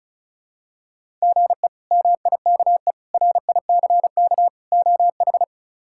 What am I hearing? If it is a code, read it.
Morse code, heard as a keyed tone, about 34 wpm, 700 Hz. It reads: GE MIKE RICK OH
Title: K1USN SST QSO Practice - 35wpm